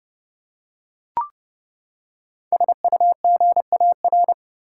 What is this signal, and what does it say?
Morse code, heard as a keyed tone, about 30 wpm, 700 Hz. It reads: SUGAR
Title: Five Letter Words - 30wpm